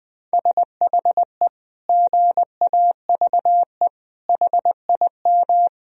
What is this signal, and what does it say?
Morse code, heard as a keyed tone, about 20 wpm, 700 Hz. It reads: SHE GAVE HIM
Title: Sentences from Top 1000 Words - Review Part 1 - 20wpm